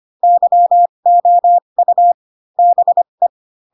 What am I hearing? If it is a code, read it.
Morse code, heard as a keyed tone, about 25 wpm, spaced as for 22 wpm, 700 Hz. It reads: YOU BE
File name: Pinocchio / Pinocchio ch6 18 25 ik1hge